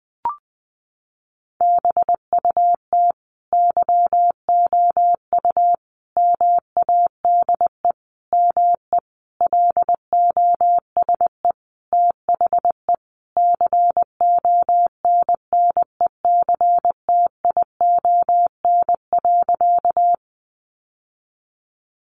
Morse code, heard as a keyed tone, about 20 wpm, 700 Hz. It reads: BUT YOU MADE ME LOSE THE CONNECTION.
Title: Sentences from Top 1300 Words 20wpm